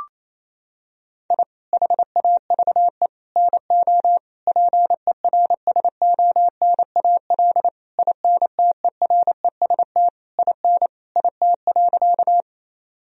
Morse code, heard as a keyed tone, about 28 wpm, 700 Hz. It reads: I HAVE NO PERSONAL INTEREST IN IT.
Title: Sentences from Top 1000 Words - Review Part 4 - 28wpm